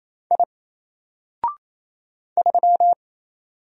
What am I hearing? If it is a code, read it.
Morse code, heard as a keyed tone, about 27 wpm, 700 Hz. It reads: I 3
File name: Single-Letter-Number-28wpm